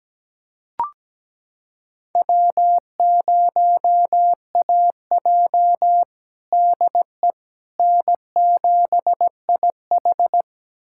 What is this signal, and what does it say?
Morse code, heard as a keyed tone, about 17 wpm, 700 Hz. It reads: W0AJ DE N7IH
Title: USA Callsign DE Callsign - 4 Characters - 17wpm